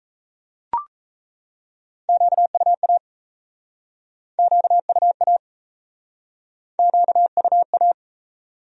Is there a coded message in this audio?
Yes: Morse code from 2.1 to 7.9 s, heard as a keyed tone, about 37 wpm, 700 Hz. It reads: QUA QUA QUA